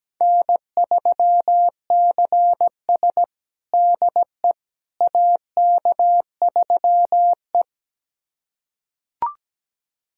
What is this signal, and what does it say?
Morse code, heard as a keyed tone, about 17 wpm, 700 Hz. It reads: N3CS DE AK3E
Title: USA Callsign DE Callsign - 4 Characters - 17wpm